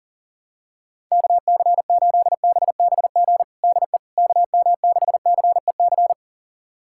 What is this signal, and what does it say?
Morse code, heard as a keyed tone, about 40 wpm, 700 Hz. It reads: KC8BBG DE KM6CEC
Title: USA Callsign DE Callsign 40wpm